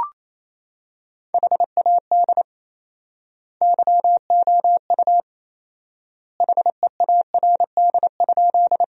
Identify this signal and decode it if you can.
Morse code, heard as a keyed tone, about 27 wpm, 700 Hz. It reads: HAD YOU HEARD?
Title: 4x Word-Spacing - Sentences from Top 400 Words - 28wpm